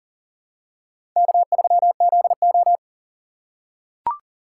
Morse code, heard as a keyed tone, about 40 wpm, 700 Hz. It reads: K3ZO
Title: USA Call Signs - 4 Characters - 3x Character Spacing - 40wpm